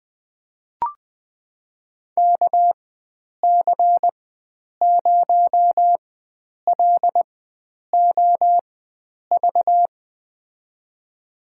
Morse code, heard as a keyed tone, about 20 wpm, spaced as for 9 wpm, 700 Hz. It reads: KC0LOV
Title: USA Call Signs - 6 Characters - 4x Character Spacing - 20wpm